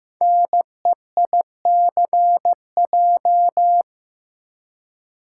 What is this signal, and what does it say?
Morse code, heard as a keyed tone, about 15 wpm, 700 Hz. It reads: NEICJ
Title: Code Groups - 5 Letters - 15wpm